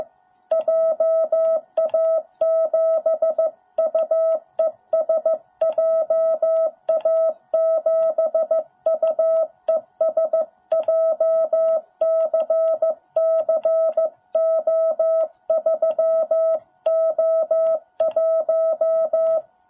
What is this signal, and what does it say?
Morse code, heard as a keyed tone, about 14 wpm, 660 Hz. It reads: JA7UES JA7UES JCCO3O1
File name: jh8cbh / 2021cw